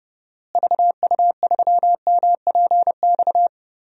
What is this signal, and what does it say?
Morse code, heard as a keyed tone, about 32 wpm, 700 Hz. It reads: VU3MPX
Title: DXCC-100 Call Signs - 6 Characters - Repeated 2x - 30wpm